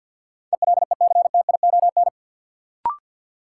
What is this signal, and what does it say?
Morse code, heard as a keyed tone, about 53 wpm, 700 Hz. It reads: ELECTION
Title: Sentences from Top 1500 Words - New Word Drill - 50wpm